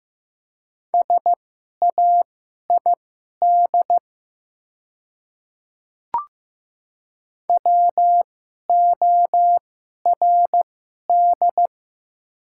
Morse code, heard as a keyed tone, about 15 wpm, spaced as for 11 wpm, 700 Hz. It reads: SAID WORD
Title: 2x Character Spacing - Top 100 Words - 15wpm